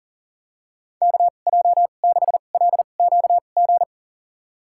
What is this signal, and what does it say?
Morse code, heard as a keyed tone, about 40 wpm, spaced as for 29 wpm, 700 Hz. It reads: KJ6LQG
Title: USA Call Signs - 6 Characters - 2x Character Spacing - 40wpm